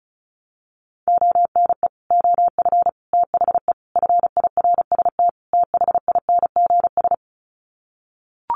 Morse code, heard as a keyed tone, about 34 wpm, 700 Hz. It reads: ONE OF THE FIRST THINGS
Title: Sets of 5 Words Encore 35wpm